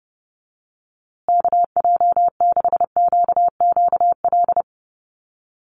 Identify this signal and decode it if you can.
Morse code, heard as a keyed tone, about 29 wpm, 700 Hz. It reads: KJ6QQL
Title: USA Call Signs Repeated 3x 30wpm